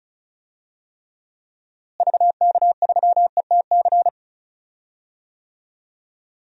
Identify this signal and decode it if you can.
Morse code, heard as a keyed tone, about 36 wpm, 700 Hz. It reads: VK3ETC
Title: DXCC-100 Call Signs - 6 Characters - Repeated 3x - 35wpm